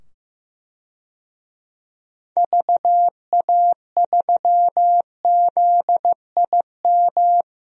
Morse code, heard as a keyed tone, about 15 wpm, 700 Hz. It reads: VA3ZIM
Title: DXCC-100 Call Signs - 6 Characters - 15wpm